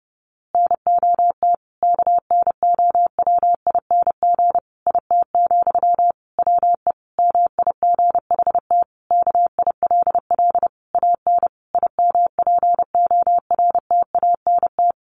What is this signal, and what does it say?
Morse code, heard as a keyed tone, about 30 wpm, 700 Hz. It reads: NOT KNOWING IT, WE MIGHT KILL AN IMPORTANT
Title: A Sound Of Thunder 30wpm